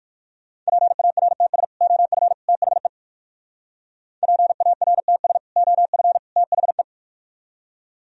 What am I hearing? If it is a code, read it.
Morse code, heard as a keyed tone, about 64 wpm, spaced as for 56 wpm, 700 Hz. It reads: PARTS OF THE PARTS OF THE
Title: Speed-Racing - Sets of 3 Words 45wpm